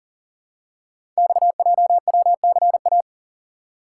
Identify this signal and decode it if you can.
Morse code, heard as a keyed tone, about 40 wpm, 700 Hz. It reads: XJWCA
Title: Code Groups - 5 Letter:Numbers - 40wpm